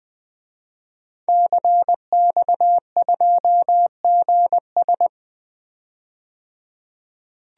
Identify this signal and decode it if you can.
Morse code, heard as a keyed tone, about 20 wpm, 700 Hz. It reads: CX2GS
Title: DXCC-100 Call Signs - 5 Characters - 20wpm